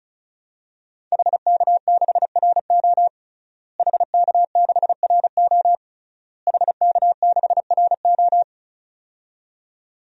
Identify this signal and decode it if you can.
Morse code, heard as a keyed tone, about 36 wpm, 700 Hz. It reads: HK6RO HK6RO HK6RO
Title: DXCC-100 Call Signs - 5 Characters - Repeated 3x - 35wpm